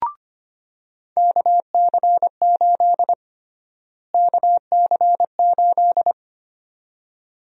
Morse code, heard as a keyed tone, about 25 wpm, 700 Hz. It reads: KC8 KC8
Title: USA Call Sign Prefixes - Repeated 2x - 25wpm